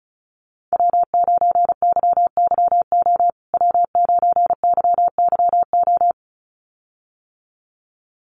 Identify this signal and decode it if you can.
Morse code, heard as a keyed tone, about 34 wpm, 700 Hz. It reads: W9YYO W9YYO
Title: USA Call Signs Repeated 2x 35wpm